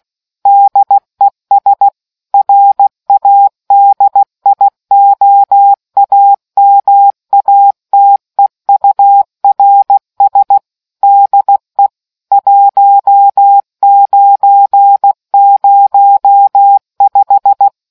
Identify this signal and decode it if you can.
Morse code, heard as a keyed tone, about 16 wpm, 800 Hz. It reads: DES RADIOAMATEURS DE 1905